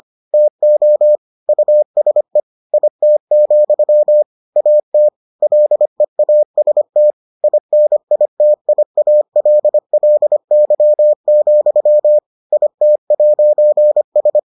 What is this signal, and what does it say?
Morse code, heard as a keyed tone, about 25 wpm, 600 Hz. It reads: TO USE IT, AT LEAST INITIALLY, IT'S